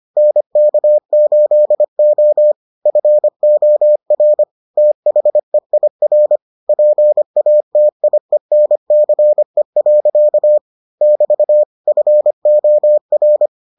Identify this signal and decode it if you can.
Morse code, heard as a keyed tone, about 25 wpm, 600 Hz. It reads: NK8O FOR THEIR PATIENCE. <BT> FOR